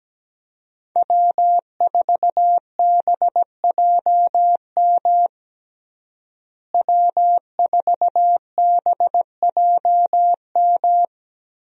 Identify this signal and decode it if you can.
Morse code, heard as a keyed tone, about 17 wpm, 700 Hz. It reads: W4BJM W4BJM
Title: USA Call Signs - 5 Characters - Repeated 3x - 17wpm